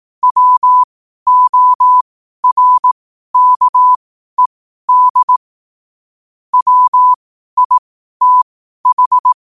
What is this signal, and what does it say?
Morse code, heard as a keyed tone, about 18 wpm, spaced as for 12 wpm, 1000 Hz. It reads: WORKED WITH
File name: qotdfiles / 12wpm